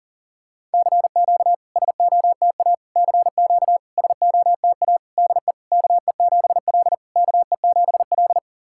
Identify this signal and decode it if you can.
Morse code, heard as a keyed tone, about 44 wpm, 690 Hz. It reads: CQ SOTA CQ SOTA DE KE7L KE7L
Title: SOTA QSO Practice 40wpm